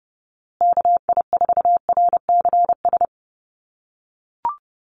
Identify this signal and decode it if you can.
Morse code, heard as a keyed tone, about 30 wpm, 700 Hz. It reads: KI4RCS
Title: USA Call Signs Repeated 2x 30wpm